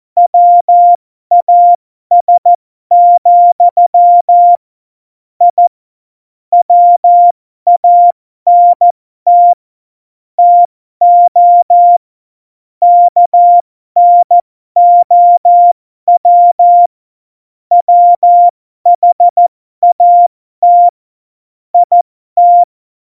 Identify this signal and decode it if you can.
Morse code, heard as a keyed tone, about 14 wpm, spaced as for 12 wpm, 700 Hz. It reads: WAS, I WANT TO KNOW WHAT IT
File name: Morsekurs / Witze